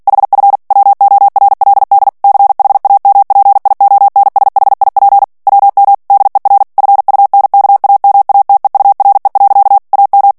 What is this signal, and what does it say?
Morse code, heard as a keyed tone, about 48 wpm, 800 Hz. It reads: HF WORLD CHAMPIONSHIP WM DER FUNKAMATEURE. AM
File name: morse mp3 / ALLEMAND texte clair 2 vitesse 45 mots minute